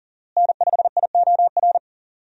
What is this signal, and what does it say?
Morse code, heard as a keyed tone, about 40 wpm, 700 Hz. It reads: N5IOR
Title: USA Call Signs - 5 Characters - Repeated 3x - 40wpm